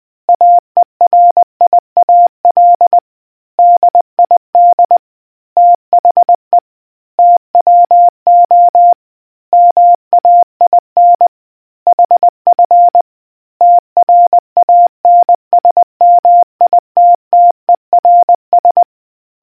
Morse code, heard as a keyed tone, about 20 wpm, 700 Hz. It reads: AERIAL DID THE TWO MAIN HF TRANSMITTERS